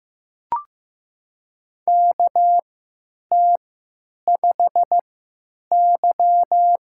Morse code, heard as a keyed tone, about 15 wpm, spaced as for 9 wpm, 700 Hz. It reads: KT5Y
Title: USA Call Signs - 4 Characters - 3x Character Spacing - 15wpm